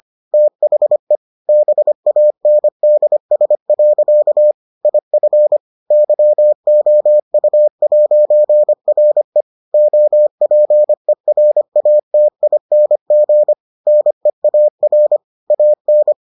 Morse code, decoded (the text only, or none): THE BANDS. IF YOU'RE OPERATING NEAR AN